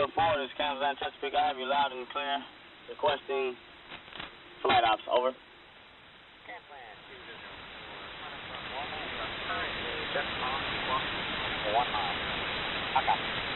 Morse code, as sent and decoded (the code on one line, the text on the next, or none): none